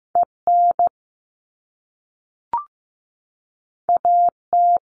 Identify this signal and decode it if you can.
Morse code, heard as a keyed tone, about 15 wpm, 700 Hz. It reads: EN AT
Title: All Cut Numbers - 2-Digit 15wpm